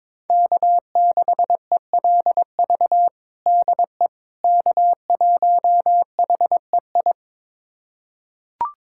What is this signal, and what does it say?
Morse code, heard as a keyed tone, about 22 wpm, 700 Hz. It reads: K6ELV DE K1HEI
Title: USA Callsign DE Callsign - 5 Characters - 22wpm